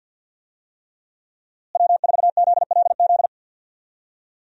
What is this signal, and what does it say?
Morse code, heard as a keyed tone, about 52 wpm, 700 Hz. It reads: W4ZLZ
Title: USA Call Signs - 5 Characters - Repeated 3x - 50wpm